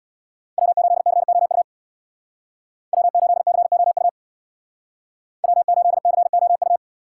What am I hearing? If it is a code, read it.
Morse code, heard as a keyed tone, about 64 wpm, 700 Hz. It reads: W8COU W8COU W8COU